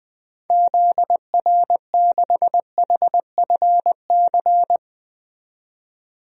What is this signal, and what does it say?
Morse code, heard as a keyed tone, about 20 wpm, 700 Hz. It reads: ZR6HFC
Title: DXCC-100 Call Signs - 6 Characters - 20wpm